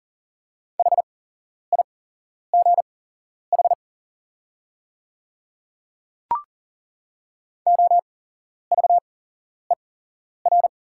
Morse code, heard as a keyed tone, about 40 wpm, spaced as for 11 wpm, 690 Hz. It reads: HIGH OVER
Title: 8x Character Spacing - Top 200 Words - 40wpm